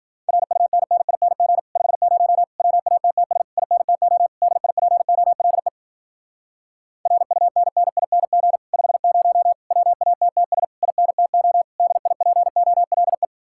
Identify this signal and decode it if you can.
Morse code, heard as a keyed tone, about 52 wpm, 700 Hz. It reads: RUNNING 50 WATTS INTO DIPOLE RUNNING 50 WATTS INTO DIPOLE